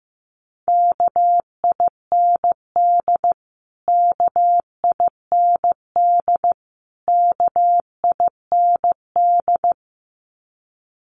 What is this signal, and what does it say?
Morse code, heard as a keyed tone, about 15 wpm, 700 Hz. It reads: KIND KIND KIND
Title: Top 200 Words Repeated 15wpm